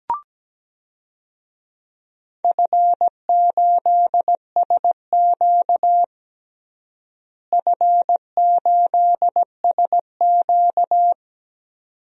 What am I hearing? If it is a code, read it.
Morse code, heard as a keyed tone, about 17 wpm, 700 Hz. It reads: F8SQ F8SQ
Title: DXCC-100 Call Signs - 4 Characters - Repeated 2x - 17wpm